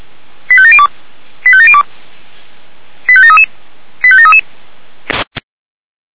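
Five-tone selective call (ZVEI), heard as quick runs of five tones, tones of about 70 ms. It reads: 85702 85702 85733 85733
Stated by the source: Received by SDR; ZVEI five-tone code 89241.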